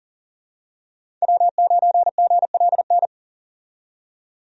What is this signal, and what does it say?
Morse code, heard as a keyed tone, about 39 wpm, 700 Hz. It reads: W9GLN
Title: USA Call Signs - 5 Characters - 40wpm